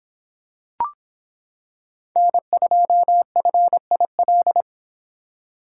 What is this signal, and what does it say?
Morse code, heard as a keyed tone, about 26 wpm, 700 Hz. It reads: N2FIL